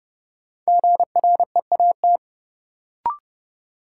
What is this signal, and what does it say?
Morse code, heard as a keyed tone, about 30 wpm, 700 Hz. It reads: GREAT
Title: Sentences from Top 200 Words - New Word Drill - 30wpm